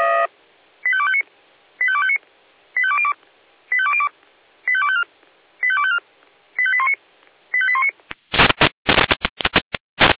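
Five-tone selective call (ZVEI), heard as quick runs of five tones, tones of about 70 ms. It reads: 85259 85259 85202 85202 85254 85254 85819 85819
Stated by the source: ZVEI five-tone selective call received on SDR.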